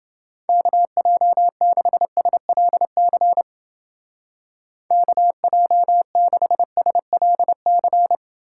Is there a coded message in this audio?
Yes: Morse code from 0.5 to 8.2 s, heard as a keyed tone, about 30 wpm, 700 Hz. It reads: KJ6SLC KJ6SLC